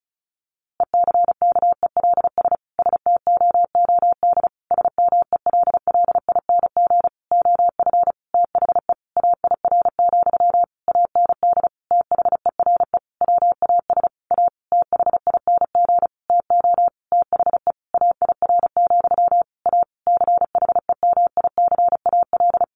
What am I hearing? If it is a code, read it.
Morse code, heard as a keyed tone, about 34 wpm, 700 Hz. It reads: ECKELS STOOD SMELLING OF THE AIR, AND THERE WAS A THING TO THE AIR, A CHEMICAL
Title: A Sound Of Thunder 35wpm